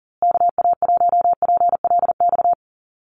Morse code, heard as a keyed tone, about 39 wpm, 700 Hz. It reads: KA1PLX
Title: USA Call Signs Repeated 3x 40wpm